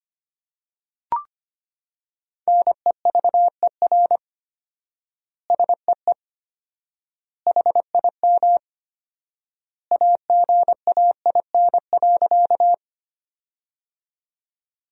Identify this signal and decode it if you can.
Morse code, heard as a keyed tone, about 25 wpm, 700 Hz. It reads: NEVER SEE HIM AGAIN.
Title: 4x Word-Spacing - Sentences from Top 300 Words - 25wpm